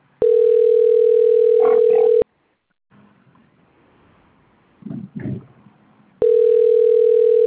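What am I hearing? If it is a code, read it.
Ringback tone (North America), 440 + 480 Hz.
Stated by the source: Outgoing phone call recorded on a call-centre PBX at Piracicaba.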